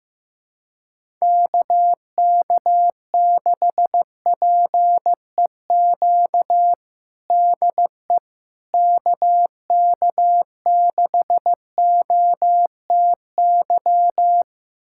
Morse code, heard as a keyed tone, about 15 wpm, 700 Hz. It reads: KK6PEQ DE KK6OTY
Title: USA Callsign DE Callsign - 6 Characters - 15wpm